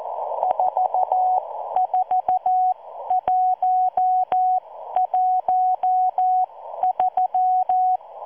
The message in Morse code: 44113